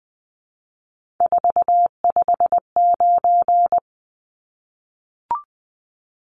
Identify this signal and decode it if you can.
Morse code, heard as a keyed tone, about 20 wpm, 700 Hz. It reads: 459
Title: Numbers - 3-Digit 20wpm